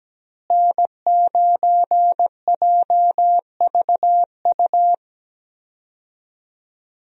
Morse code, heard as a keyed tone, about 17 wpm, 700 Hz. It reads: N9JVU
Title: USA Call Signs - 5 Characters - Repeated 2x - 17wpm